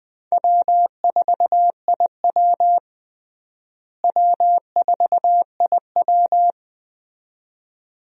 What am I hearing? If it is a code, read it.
Morse code, heard as a keyed tone, about 20 wpm, 700 Hz. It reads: W4IW W4IW
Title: USA Call Signs - 4 Characters - Repeated 2x - 20wpm